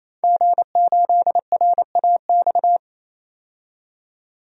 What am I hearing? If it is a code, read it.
Morse code, heard as a keyed tone, about 28 wpm, 700 Hz. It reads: G8RAX
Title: DXCC-100 Call Signs - 5 Characters - 28wpm